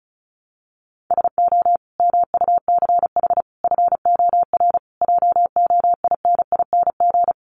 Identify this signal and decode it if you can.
Morse code, heard as a keyed tone, about 34 wpm, 700 Hz. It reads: SO MUCH FOR JOINING
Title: Sets of 5 Words 35wpm